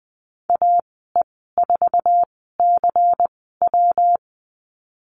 Morse code, heard as a keyed tone, about 20 wpm, spaced as for 15 wpm, 700 Hz. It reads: AE4CW
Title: 128 Lesson 21g - Number 6 - Call Signs 20wpm